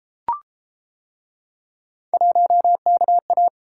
Morse code, heard as a keyed tone, about 32 wpm, 700 Hz. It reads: 1KA